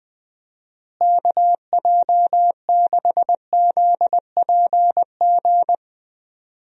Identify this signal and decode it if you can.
Morse code, heard as a keyed tone, about 20 wpm, 700 Hz. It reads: KJ6ZPG
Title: USA Call Signs - 6 Characters - Repeated 2x - 20wpm